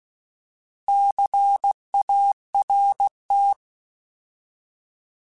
Morse code, heard as a keyed tone, about 16 wpm, 800 Hz. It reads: CART
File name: Alphabetical / Alphabetical mp3s / Cs @15